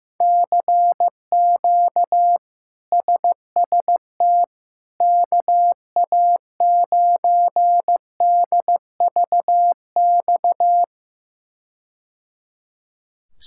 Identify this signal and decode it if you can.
Morse code, heard as a keyed tone, about 15 wpm, 700 Hz. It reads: CQ SST KA9DVX
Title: K1USN SST QSO Practice - 15wpm